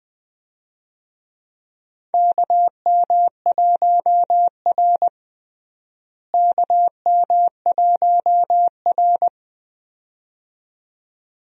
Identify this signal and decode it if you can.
Morse code, heard as a keyed tone, about 20 wpm, 700 Hz. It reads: KM1R KM1R
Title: USA Call Signs - 4 Characters - Repeated 2x - 20wpm